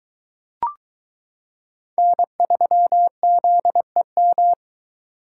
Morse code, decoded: N3ZEM

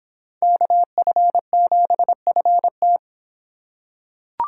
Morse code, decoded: KF7FT

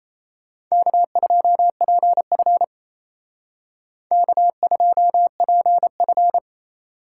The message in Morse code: K2PF K2PF

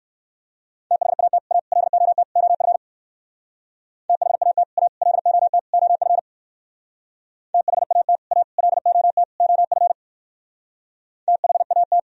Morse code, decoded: THAT A LOT OF THAT A LOT OF THAT A LOT OF THAT

